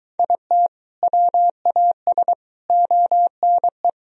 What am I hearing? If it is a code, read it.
Morse code, heard as a keyed tone, about 23 wpm, 700 Hz. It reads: IT WAS ONE